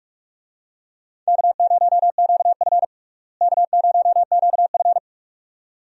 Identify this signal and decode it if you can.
Morse code, heard as a keyed tone, about 42 wpm, 700 Hz. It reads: K0QF K0QF
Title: USA Call Signs - 4 Characters - Repeated 2x - 45wpm